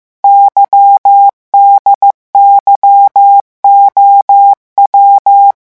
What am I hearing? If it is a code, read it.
Morse code, heard as a keyed tone, about 15 wpm, 800 Hz. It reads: YDYOW
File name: morse mp3 / revision8-vitesse 14 mots